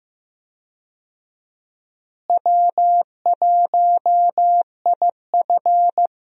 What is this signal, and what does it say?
Morse code, heard as a keyed tone, about 15 wpm, 700 Hz. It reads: W1IF
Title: USA Call Signs - 4 Characters - Repeated 3x - 15wpm